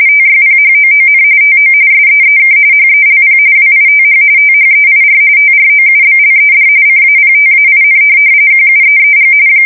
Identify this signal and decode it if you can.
RTTY, about 45.45 baud. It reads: TYPHOON TO SEEK HELP WITH THEIR TROUBLED CAPTAIN QUEEG
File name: fsk170 / USS-New-Jersey